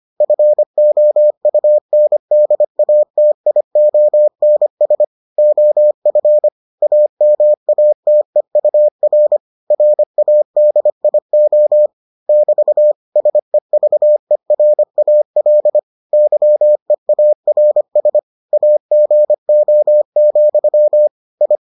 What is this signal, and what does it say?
Morse code, heard as a keyed tone, about 25 wpm, 600 Hz. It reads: FOUNDATIONS OF AMATEUR RADIO <BT> SEVERAL YEARS AGO, I